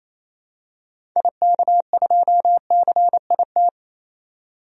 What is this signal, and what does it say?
Morse code, heard as a keyed tone, about 28 wpm, 700 Hz. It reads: IK2CIT
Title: DXCC-100 Call Signs - 6 Characters - 28wpm